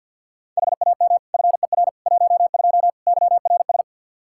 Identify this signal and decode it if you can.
Morse code, heard as a keyed tone, about 50 wpm, 700 Hz. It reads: HAM FER 12 YRS
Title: Practice QSO - Exchange 3 - More Info - 50wpm